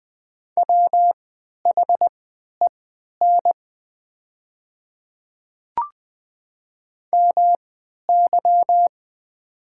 Morse code, heard as a keyed tone, about 20 wpm, spaced as for 11 wpm, 700 Hz. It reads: WHEN MY
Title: 3x Character Spacing - Top 100 Words - 20wpm